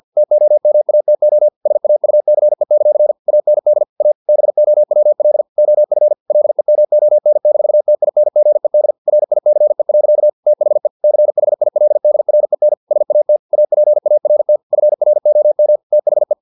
Morse code, heard as a keyed tone, about 50 wpm, 600 Hz. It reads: TOMATO SAUCE; AND A BOWL OF LEMON-TINGED RICE. THE CHILDREN SAT APART FROM THE
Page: Never 49wpm